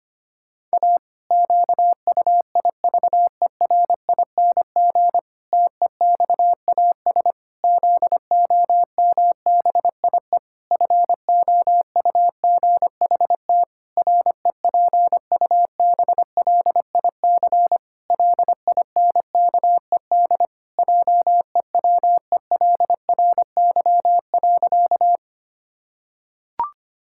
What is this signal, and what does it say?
Morse code, heard as a keyed tone, about 25 wpm, 700 Hz. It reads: A QUIVERING TEXAS ZOMBIE FOUGHT REPUBLIC LINKED JEWELRY.
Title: Pangram Challenge v2 - 25wpm